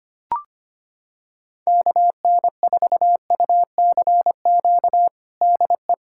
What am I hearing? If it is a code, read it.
Morse code, heard as a keyed tone, about 25 wpm, 700 Hz. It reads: KN4UCQ DE
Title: USA Callsign DE Callsign - 6 Characters - 25wpm